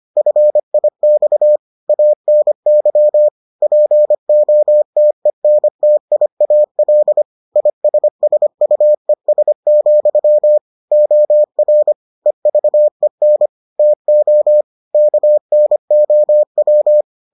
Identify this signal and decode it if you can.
Morse code, heard as a keyed tone, about 25 wpm, 600 Hz. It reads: FIX ANY POTENTIAL ISSUES, OR EVEN TO KNOW